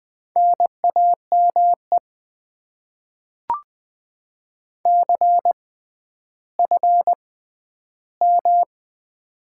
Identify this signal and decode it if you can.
Morse code, heard as a keyed tone, about 20 wpm, spaced as for 11 wpm, 700 Hz. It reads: NAME CFM
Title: Top 90 QSO Elements - 6x Character Spacing - 20wpm